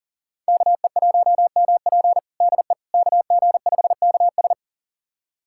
Morse code, heard as a keyed tone, about 40 wpm, 700 Hz. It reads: KE1MP DE KG5KS
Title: USA Callsign DE Callsign - 5 Characters - 40wpm